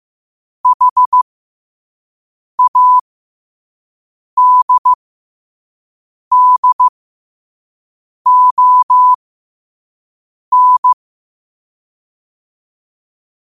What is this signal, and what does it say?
Morse code, heard as a keyed tone, about 15 wpm, spaced as for 5 wpm, 1000 Hz. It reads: HADDON
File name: qotdfiles / 5wpm